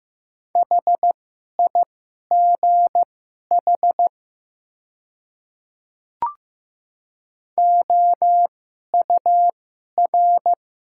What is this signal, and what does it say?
Morse code, heard as a keyed tone, about 15 wpm, spaced as for 11 wpm, 700 Hz. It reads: HIGH OUR
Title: 2x Character Spacing - Top 500 Words - 15wpm